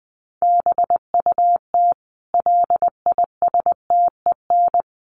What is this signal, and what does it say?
Morse code, heard as a keyed tone, about 20 wpm, 700 Hz. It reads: BUT LISTEN
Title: Sentences from Top 400 Words 20wpm